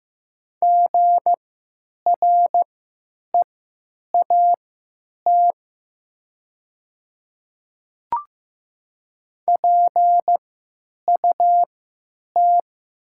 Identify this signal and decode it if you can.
Morse code, heard as a keyed tone, about 15 wpm, spaced as for 9 wpm, 700 Hz. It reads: GREAT PUT
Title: 3x Character Spacing - Top 200 Words - 15wpm